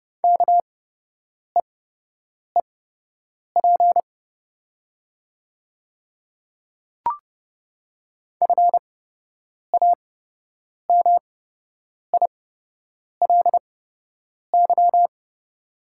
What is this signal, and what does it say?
Morse code, heard as a keyed tone, about 30 wpm, spaced as for 8 wpm, 700 Hz. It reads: KEEP FAMILY
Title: 8x Character Spacing - Top 500 Words - 30wpm